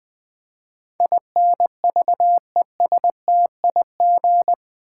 Morse code, heard as a keyed tone, about 20 wpm, 700 Hz. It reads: INVESTIG
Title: Build A Word - 4 Letters At A Time - 12-Letter Words - 20wpm